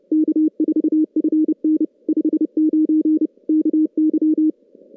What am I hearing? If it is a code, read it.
Morse code, heard as a keyed tone, about 30 wpm, 310 Hz. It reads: K4FN 59 KY
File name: contests / CW-SS-k5tr- -2300